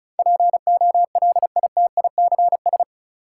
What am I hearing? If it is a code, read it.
Morse code, heard as a keyed tone, about 36 wpm, 700 Hz. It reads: POLITICS
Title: Sentences from Top 1000 Words - New Word Drill - 35wpm